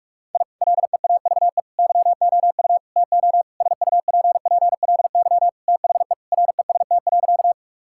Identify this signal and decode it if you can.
Morse code, heard as a keyed tone, about 41 wpm, 700 Hz. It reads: I LEAVE YOU TO SUPPLY THE REST.
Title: Sentences from Top 1000 Words - Review Part 5 - 45wpm